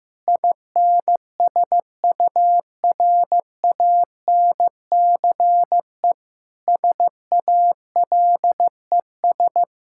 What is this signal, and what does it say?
Morse code, heard as a keyed tone, about 16 wpm, 700 Hz. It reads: INSURANCE SALES
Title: K1USN SST QSO Practice 15wpm